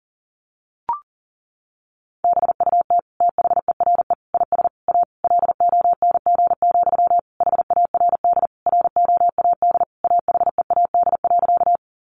Morse code, heard as a keyed tone, about 40 wpm, 700 Hz. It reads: BUT THERE IS A LONG, HARD ROAD AHEAD.
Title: Sentences from Top 1100 Words 40wpm